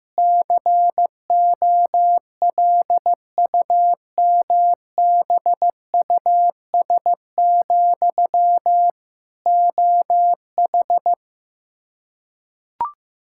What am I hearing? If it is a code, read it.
Morse code, heard as a keyed tone, about 15 wpm, 700 Hz. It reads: COLUMBUS, OH
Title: K1USN SST QSO Practice 15wpm